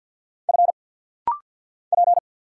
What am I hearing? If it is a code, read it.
Morse code, heard as a keyed tone, about 50 wpm, 700 Hz. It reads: F P
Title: Single Letter-Number Alt - Rapid-Fire 50wpm